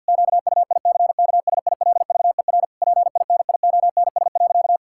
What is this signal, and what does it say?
Morse code, heard as a keyed tone, about 51 wpm, 700 Hz. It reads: QUICKSILVER PINIONS.
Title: Pangram Challenge v2 - 50wpm